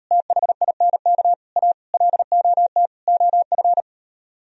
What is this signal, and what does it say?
Morse code, heard as a keyed tone, about 35 wpm, 700 Hz. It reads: THINK A LOT OF
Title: Speed-Racing - Sets of 5 Words 28wpm